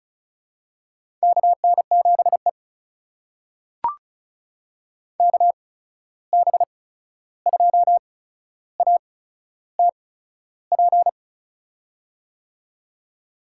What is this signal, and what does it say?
Morse code, heard as a keyed tone, about 36 wpm, spaced as for 13 wpm, 700 Hz. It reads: KN7E KB2ATP
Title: USA Call Signs - 8x Character Spacing - 35wpm